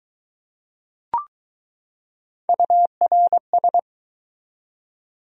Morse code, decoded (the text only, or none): URS